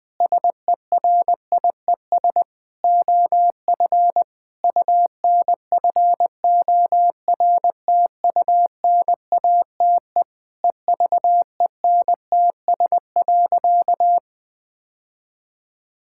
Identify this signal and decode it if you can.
Morse code, heard as a keyed tone, about 21 wpm, 700 Hz. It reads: SERIES OF UNFORTUNATE EVENTS.
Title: Sentences with 8-Word Phrases - 20wpm